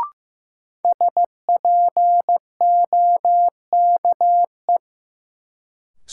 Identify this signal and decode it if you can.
Morse code, heard as a keyed tone, about 15 wpm, 700 Hz. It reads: SPOKE
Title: Sentences from Top 1200 Words - New Word Drill Alt - 15wpm